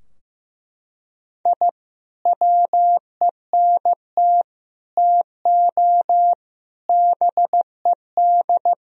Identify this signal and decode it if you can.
Morse code, heard as a keyed tone, about 15 wpm, 700 Hz. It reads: I WENT TO BED
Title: Sentences from Top 1000 Words - Review Part 2 - 15wpm